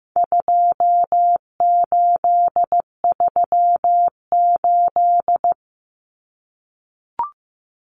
Morse code, heard as a keyed tone, about 15 wpm, 700 Hz. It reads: 2838
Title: Common Cut Numbers - 4-Digit 15wpm